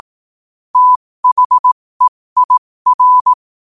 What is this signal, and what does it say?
Morse code, heard as a keyed tone, about 18 wpm, spaced as for 15 wpm, 1000 Hz. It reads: THEIR
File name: qotdfiles / 15wpm